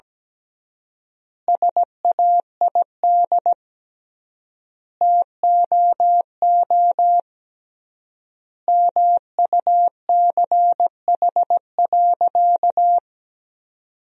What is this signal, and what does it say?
Morse code, heard as a keyed tone, about 17 wpm, 700 Hz. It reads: SAID TOO MUCH.
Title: 3x Word-Spacing - Sentences from Top 200 Words - 17wpm